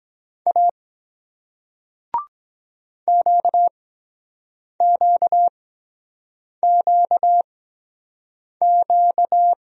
Morse code, heard as a keyed tone, about 24 wpm, spaced as for 7 wpm, 700 Hz. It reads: A QQQQ